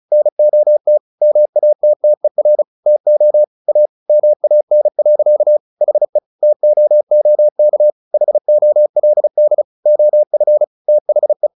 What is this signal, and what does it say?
Morse code, heard as a keyed tone, about 34 wpm, 600 Hz. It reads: NOT MATTER TO A MAN. HE TOOK HOLD OF THE
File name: the old man and the sea 2 35wpm